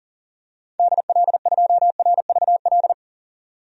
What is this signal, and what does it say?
Morse code, heard as a keyed tone, about 37 wpm, 700 Hz. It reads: DL2RVL
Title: DXCC-100 Call Signs - 6 Characters - 40wpm